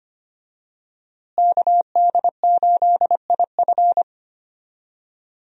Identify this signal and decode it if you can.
Morse code, heard as a keyed tone, about 25 wpm, 700 Hz. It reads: KD8IF
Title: USA Call Signs - 5 Characters - Repeated 2x - 25wpm